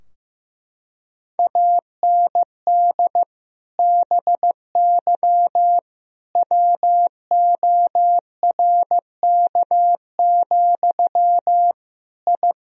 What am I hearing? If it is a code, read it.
Morse code, heard as a keyed tone, about 15 wpm, 700 Hz. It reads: AND BY WORK, I
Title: Sentences from Top 1000 Words - Review Part 4 - 15wpm